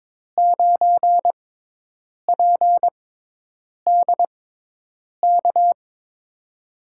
Morse code, heard as a keyed tone, about 22 wpm, spaced as for 8 wpm, 700 Hz. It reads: 9PDK
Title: USA Call Signs - 6x Character Spacing - 22wpm